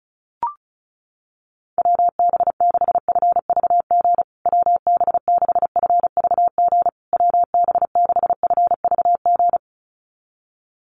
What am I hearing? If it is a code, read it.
Morse code, heard as a keyed tone, about 34 wpm, 700 Hz. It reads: WB6FVG WB6FVG WB6FVG
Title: USA Call Signs Repeated 3x 35wpm